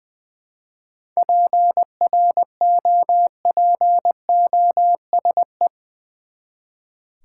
Morse code, heard as a keyed tone, about 20 wpm, 700 Hz. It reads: PROPOSE
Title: Seven Letter Words - 20wpm